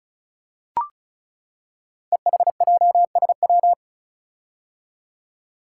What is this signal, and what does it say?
Morse code, heard as a keyed tone, about 36 wpm, 700 Hz. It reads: EHJSW
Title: Code Groups - 5 Letters - 35wpm